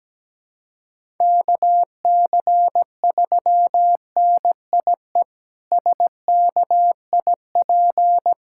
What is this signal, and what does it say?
Morse code, heard as a keyed tone, about 17 wpm, 700 Hz. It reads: KC3NIE SKIP
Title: K1USN SST QSO Practice - 17wpm